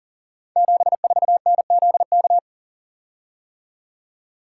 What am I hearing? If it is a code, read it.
Morse code, heard as a keyed tone, about 42 wpm, 700 Hz. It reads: 74NZK
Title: Code Groups - 5 Letter:Numbers - 40wpm